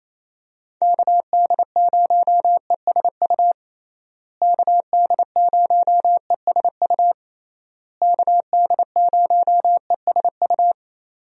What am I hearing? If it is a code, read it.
Morse code, heard as a keyed tone, about 27 wpm, 700 Hz. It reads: KD0ESU KD0ESU KD0ESU
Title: USA Call Signs - 6 Characters - Repeated 3x - 28wpm